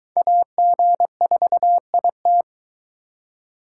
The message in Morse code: AG4IT

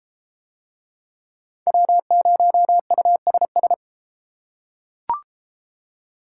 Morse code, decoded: W0USS

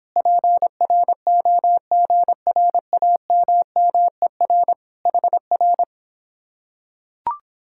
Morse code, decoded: PROGRAMMER HR

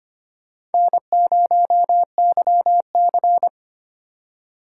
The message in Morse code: N0YC

